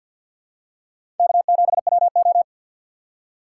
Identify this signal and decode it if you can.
Morse code, heard as a keyed tone, about 53 wpm, 700 Hz. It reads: K7WO
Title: USA Call Signs - 4 Characters - 2x Character Spacing - 50wpm